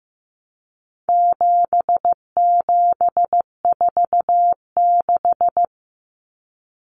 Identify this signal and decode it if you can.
Morse code, heard as a keyed tone, about 15 wpm, 700 Hz. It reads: 7746
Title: Numbers - 4-Digit 15wpm